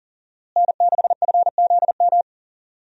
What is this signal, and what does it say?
Morse code, heard as a keyed tone, about 40 wpm, 700 Hz. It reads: N6FZM
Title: USA Call Signs - 5 Characters - Repeated 3x - 40wpm